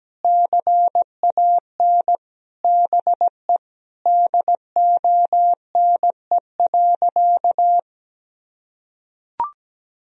Morse code, heard as a keyed tone, about 17 wpm, 700 Hz. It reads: CAN BE DONE.
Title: Sentences from Top 1000 Words - Review Part 1 - 17wpm